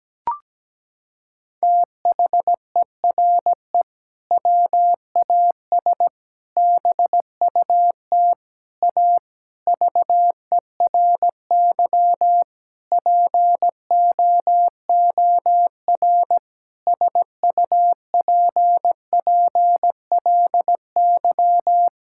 Morse code, heard as a keyed tone, about 17 wpm, 700 Hz. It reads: THERE WAS BUT A VERY POOR SUPPLY
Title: Sentences from Top 1000 Words - Review Part 5 - 17wpm